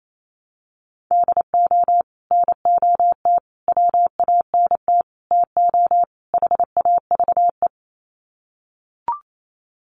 Morse code, heard as a keyed tone, about 28 wpm, 700 Hz. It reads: DO NOT WANT TO HAVE
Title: Sets of 5 Words 28wpm